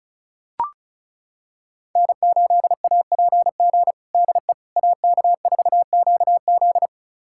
Morse code, heard as a keyed tone, about 36 wpm, 700 Hz. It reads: N8APG DE AK4QZ
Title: USA Callsign DE Callsign - 5 Characters - 35wpm